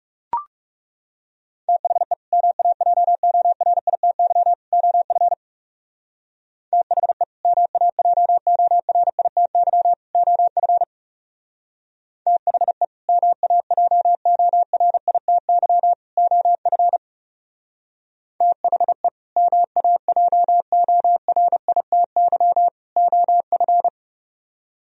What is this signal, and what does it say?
Morse code, heard as a keyed tone, about 37 wpm, 700 Hz. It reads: THE MAJORITY OF THE MAJORITY OF THE MAJORITY OF THE MAJORITY OF
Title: Speed-Racing - Sets of 3 Words 30wpm